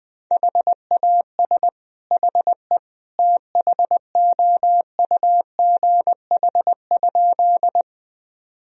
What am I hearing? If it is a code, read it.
Morse code, heard as a keyed tone, about 20 wpm, 700 Hz. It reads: HAS HE THOUGH?
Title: Sentences from Top 1000 Words - Review Part 2 - 20wpm